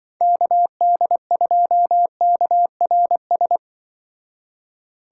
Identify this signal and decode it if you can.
Morse code, heard as a keyed tone, about 24 wpm, 700 Hz. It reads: KD2KRS
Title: Speed Racing - USA 2x3 Call Signs 20wpm